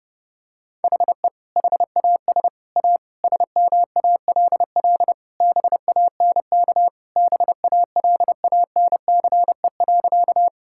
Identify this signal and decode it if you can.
Morse code, heard as a keyed tone, about 32 wpm, 700 Hz. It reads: HE HAS A SMALL BANK BALANCE.
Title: Sentences from Top 1000 Words - Review Part 6 - 30wpm